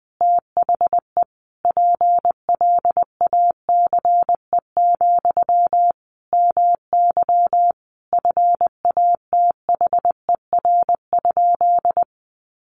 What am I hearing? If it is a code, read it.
Morse code, heard as a keyed tone, about 20 wpm, 700 Hz. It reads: THE PLACE, MY FATHER?
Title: Sentences from Top 400 Words 20wpm